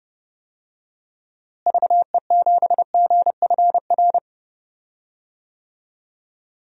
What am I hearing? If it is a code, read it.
Morse code, heard as a keyed tone, about 30 wpm, 700 Hz. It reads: VE7GFR
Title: DXCC-100 Call Signs - 6 Characters - Repeated 3x - 30wpm